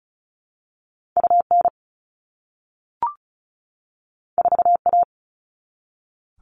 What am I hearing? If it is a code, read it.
Morse code, heard as a keyed tone, about 36 wpm, 700 Hz. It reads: UN 4A
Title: All Cut Numbers - 2-Digit 35wpm